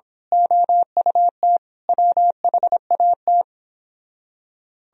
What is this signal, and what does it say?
Morse code, heard as a keyed tone, about 26 wpm, 700 Hz. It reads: OUT WHAT
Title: Speed-Racing - Sets of 5 Words 22wpm